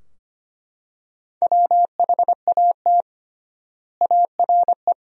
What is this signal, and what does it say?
Morse code, heard as a keyed tone, about 25 wpm, 700 Hz. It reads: WHAT ARE
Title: 3x Word-Spacing - Sentences from Top 100 Words - 25wpm